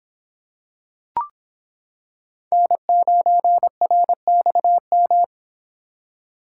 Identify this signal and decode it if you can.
Morse code, heard as a keyed tone, about 25 wpm, 700 Hz. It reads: N9RXM